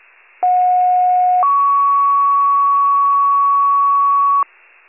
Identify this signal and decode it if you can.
Two-tone page, heard as a one-second tone, then a three-second tone, about 730 then 1120 Hz. It is heard from 0.4 to 4.4 s.